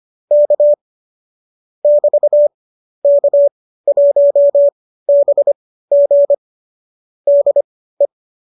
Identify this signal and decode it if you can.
Morse code, heard as a keyed tone, about 25 wpm, spaced as for 15 wpm, 590 Hz. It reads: K <BT> K1BG DE
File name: qso201 15